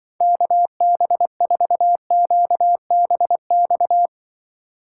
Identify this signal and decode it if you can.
Morse code, heard as a keyed tone, about 25 wpm, 700 Hz. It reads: KB4QBX
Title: Speed Racing - USA 2x3 Call Signs 20wpm